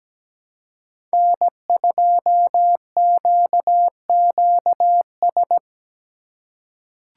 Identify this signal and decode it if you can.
Morse code, heard as a keyed tone, about 17 wpm, 700 Hz. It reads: N2QQS
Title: USA Call Signs - 5 Characters - 6x Character Spacing - 17wpm